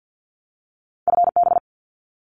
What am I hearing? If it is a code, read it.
Morse code, heard as a keyed tone, about 50 wpm, 700 Hz. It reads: FB
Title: Abbreviations Repeated 3x - 50wpm